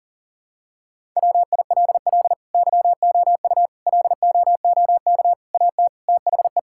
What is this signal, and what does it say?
Morse code, heard as a keyed tone, about 40 wpm, 700 Hz. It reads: WILL YOU LOOK AT THE
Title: Sentences from Top 1000 Words - Review Part 5 - 40wpm